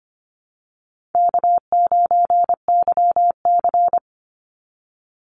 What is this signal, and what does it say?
Morse code, heard as a keyed tone, about 25 wpm, 700 Hz. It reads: K9YC
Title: Easy Call Signs - 25wpm